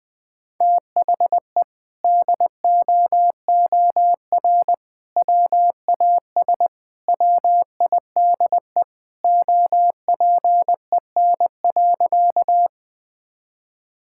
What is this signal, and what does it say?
Morse code, heard as a keyed tone, about 20 wpm, 700 Hz. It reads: THE DOOR WAS WIDE OPEN.
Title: Sentences from Top 1000 Words - Review Part 6 - 20wpm